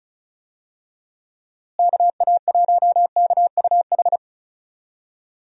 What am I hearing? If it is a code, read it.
Morse code, heard as a keyed tone, about 34 wpm, 700 Hz. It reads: KA1KUH
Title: USA Call Signs - 6 Characters - Repeated 3x - 35wpm